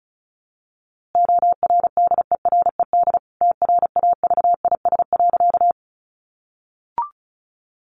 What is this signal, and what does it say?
Morse code, heard as a keyed tone, about 34 wpm, 700 Hz. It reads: ORDERED TRAVIS.
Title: A Sound Of Thunder 35wpm